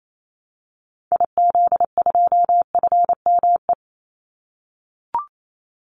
Morse code, heard as a keyed tone, about 28 wpm, 700 Hz. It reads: IZ2FME
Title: Call Signs - 28wpm